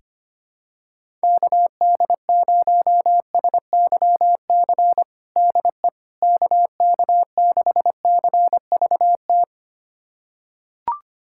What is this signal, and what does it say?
Morse code, heard as a keyed tone, about 25 wpm, 700 Hz. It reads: KD0SYC DE KK6CVT
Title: USA Callsign DE Callsign 25wpm